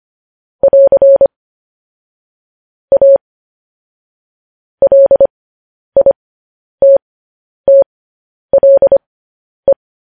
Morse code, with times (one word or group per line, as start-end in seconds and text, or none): 0.6-1.3 <AR>
2.9-3.2 A
4.8-9.7 LITTLE